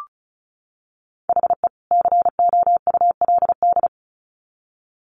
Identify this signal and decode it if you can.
Morse code, heard as a keyed tone, about 34 wpm, 700 Hz. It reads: HE COULD
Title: Sets of 2 Words 35wpm